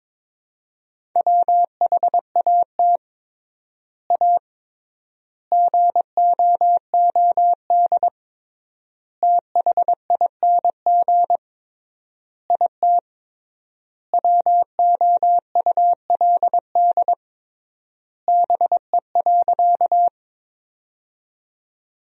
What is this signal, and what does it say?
Morse code, heard as a keyed tone, about 22 wpm, 700 Hz. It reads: WHAT A GOOD THING IT WOULD BE.
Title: 3x Word-Spacing - Sentences from Top 200 Words - 22wpm